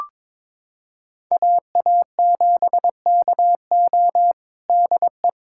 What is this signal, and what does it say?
Morse code, heard as a keyed tone, about 21 wpm, 700 Hz. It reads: AA7KO DE
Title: USA Callsign DE Callsign - 5 Characters - 22wpm